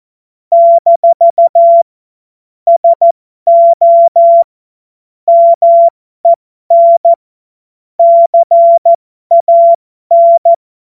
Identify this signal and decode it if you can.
Morse code, heard as a keyed tone, about 14 wpm, spaced as for 12 wpm, 680 Hz. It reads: - SO MEN CAN
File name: Morsekurs / Witze